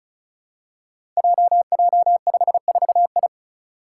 Morse code, heard as a keyed tone, about 36 wpm, 700 Hz. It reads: JJ54I
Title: Five Challenging Letter-Number Combinations 35wpm